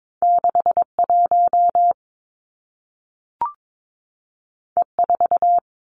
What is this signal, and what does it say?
Morse code, heard as a keyed tone, about 22 wpm, 700 Hz. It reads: 6 1 E 4
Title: Two Challenging Letter-Number Combinations 22wpm